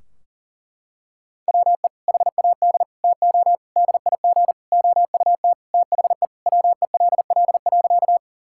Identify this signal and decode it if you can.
Morse code, heard as a keyed tone, about 40 wpm, 700 Hz. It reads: WE HAD TO DIG OUT THE WELL.
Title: Sentences from Top 1600 Words 40wpm